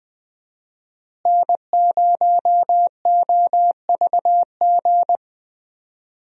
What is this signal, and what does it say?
Morse code, heard as a keyed tone, about 20 wpm, 700 Hz. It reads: N0OVG
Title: USA Call Signs - 5 Characters - Repeated 3x - 20wpm